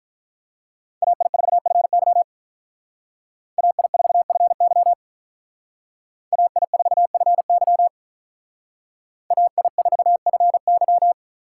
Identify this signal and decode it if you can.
Morse code, heard as a keyed tone, about 45 wpm, 700 Hz. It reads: AI4FY AI4FY AI4FY AI4FY